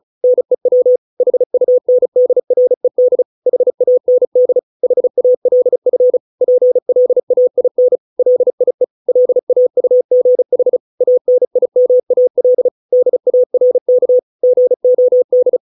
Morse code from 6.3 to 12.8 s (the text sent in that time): PLAIN LIE LAUGH ANIMAL